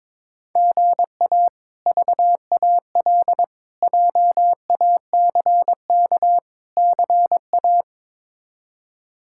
Morse code, heard as a keyed tone, about 22 wpm, 700 Hz. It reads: GA VAL JACK CA
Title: K1USN SST QSO Practice - 22wpm